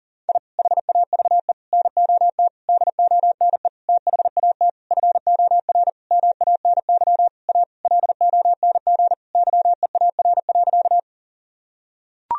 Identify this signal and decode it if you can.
Morse code, heard as a keyed tone, about 40 wpm, 700 Hz. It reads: I HAVE NOT DONE THAT FOR MANY A LONG YEAR.
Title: Sentences from Top 1000 Words - Review Part 1 - 40wpm